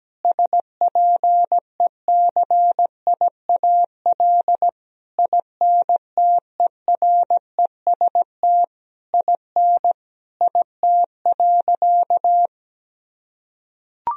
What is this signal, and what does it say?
Morse code, heard as a keyed tone, about 17 wpm, 700 Hz. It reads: SPECIAL INTEREST IN IT.
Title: Sentences from Top 1000 Words - Review Part 4 - 17wpm